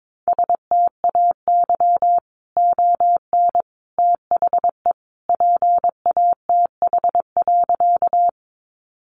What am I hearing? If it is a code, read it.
Morse code, heard as a keyed tone, about 22 wpm, 700 Hz. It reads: STAY ON THE PATH.
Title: A Sound Of Thunder 22wpm